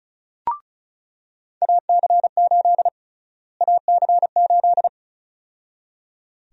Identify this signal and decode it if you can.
Morse code, heard as a keyed tone, about 34 wpm, 700 Hz. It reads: AC8 AC8
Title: USA Call Sign Prefixes - Repeated 2x - 35wpm